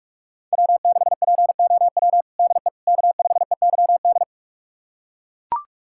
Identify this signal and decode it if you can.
Morse code, heard as a keyed tone, about 42 wpm, 700 Hz. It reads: W6POW DE K5EYD
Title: USA Callsign DE Callsign 45wpm